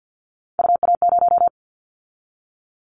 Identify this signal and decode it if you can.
Morse code, heard as a keyed tone, about 50 wpm, 700 Hz. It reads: UA0
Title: Callsign Prefixes for Top 100 DXCC - 50wpm